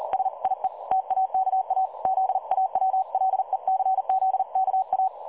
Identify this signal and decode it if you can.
Morse code, heard as a keyed tone, about 38 wpm, 760 Hz. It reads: SIE TAKA ZAWLECZKA
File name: dlazaawansowanych / nagraniazpasma / websdr recording 3565.0kHz